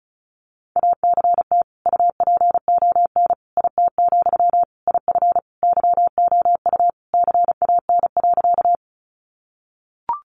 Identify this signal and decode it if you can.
Morse code, heard as a keyed tone, about 34 wpm, 700 Hz. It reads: ACT UPON IT, IF YOU CAN.
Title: Sentences from Top 1100 Words 35wpm